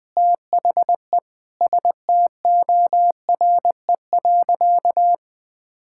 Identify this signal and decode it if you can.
Morse code, heard as a keyed tone, about 20 wpm, 700 Hz. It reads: THE STORE.
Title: Sentences from Top 1000 Words - Review Part 5 - 20wpm